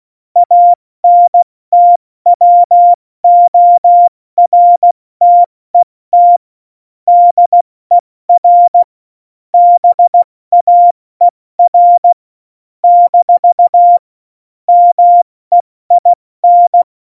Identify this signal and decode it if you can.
Morse code, heard as a keyed tone, about 16 wpm, spaced as for 14 wpm, 700 Hz. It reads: ANTWORTET DER BAER - MEIN
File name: Morsekurs / Witze / Witz 17.Juli 2018, 70 F80